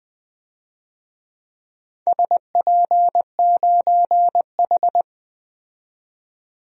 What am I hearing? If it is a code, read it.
Morse code, heard as a keyed tone, about 20 wpm, 700 Hz. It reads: SP9H
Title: DXCC-100 Call Signs - 4 Characters - 20wpm